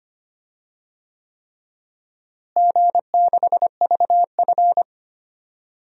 Morse code, heard as a keyed tone, about 25 wpm, 700 Hz. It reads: G6VF
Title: DXCC-100 Call Signs - 4 Characters - Repeated 3x - 25wpm